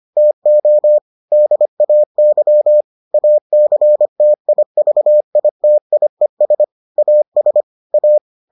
Morse code, heard as a keyed tone, about 25 wpm, 600 Hz. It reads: TO DAY ACTIVITIES AS A